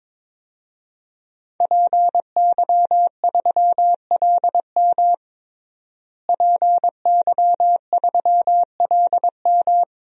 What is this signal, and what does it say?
Morse code, heard as a keyed tone, about 21 wpm, 700 Hz. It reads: PY3LM PY3LM
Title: DXCC-100 Call Signs - 5 Characters - Repeated 3x - 22wpm